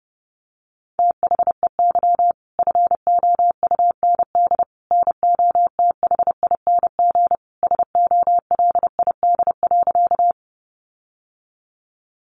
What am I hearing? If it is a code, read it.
Morse code, heard as a keyed tone, about 30 wpm, 700 Hz. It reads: THEY FOUND NOTHING SOLID.
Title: Sentences from Top 1500 Words 30wpm